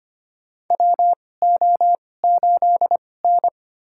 Morse code, heard as a keyed tone, about 25 wpm, spaced as for 18 wpm, 700 Hz. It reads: WO8N
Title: USA Call Signs - 2x Character Spacing - 25wpm